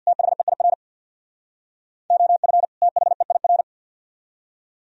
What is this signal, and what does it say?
Morse code, heard as a keyed tone, about 64 wpm, 700 Hz. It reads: THEIR OF THEIR